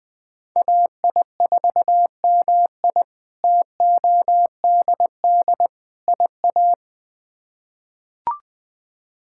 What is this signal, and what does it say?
Morse code, heard as a keyed tone, about 20 wpm, 700 Hz. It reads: AI4MI TODD IA
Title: K1USN SST QSO Practice - 20wpm